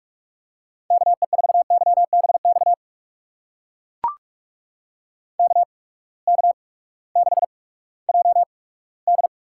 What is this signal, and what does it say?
Morse code, heard as a keyed tone, about 44 wpm, spaced as for 20 wpm, 700 Hz. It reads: KE4YBX KK6JD
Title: USA Call Signs - 6 Characters - 8x Character Spacing - 45wpm